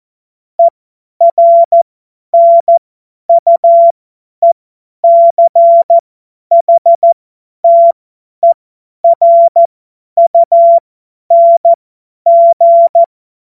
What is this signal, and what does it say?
Morse code, heard as a keyed tone, about 14 wpm, spaced as for 10 wpm, 680 Hz. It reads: ERNUECHTERUNG
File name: Morsekurs / Witze / Witz 09. Dezember 2025 50 F70